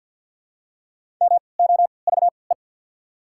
Morse code, heard as a keyed tone, about 48 wpm, spaced as for 28 wpm, 690 Hz. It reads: MOVE
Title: 3x Character Spacing - Top 300 Words - 50wpm